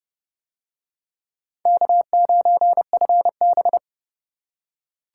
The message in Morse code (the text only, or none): K9FB